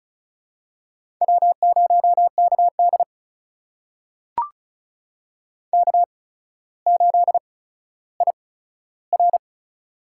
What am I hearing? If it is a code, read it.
Morse code, heard as a keyed tone, about 35 wpm, spaced as for 15 wpm, 700 Hz. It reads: W0KD K8IR
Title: USA Call Signs - 4 Characters - 8x Character Spacing - 35wpm